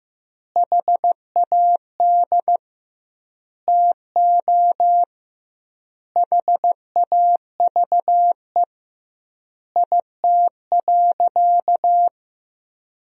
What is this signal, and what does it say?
Morse code, heard as a keyed tone, about 16 wpm, 700 Hz. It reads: HAD TO HAVE IT.
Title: 2x Word-Spacing - Sentences from Top 100 Words - 15wpm